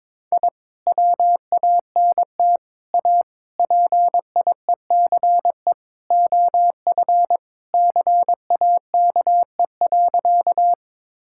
Morse code, heard as a keyed tone, about 22 wpm, 700 Hz. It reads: I WANT A PIECE OF CAKE.
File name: practice / 22wpm / Sentences with 3-Word Phrases - 22wpm